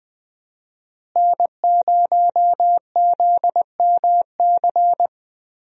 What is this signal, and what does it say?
Morse code, heard as a keyed tone, about 20 wpm, 700 Hz. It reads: N0ZMC
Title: USA Call Signs - 5 Characters - Repeated 3x - 20wpm